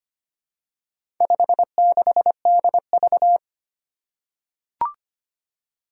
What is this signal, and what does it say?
Morse code, heard as a keyed tone, about 25 wpm, 700 Hz. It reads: 56DV
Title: Four Challenging Letter-Number Combinations 25wpm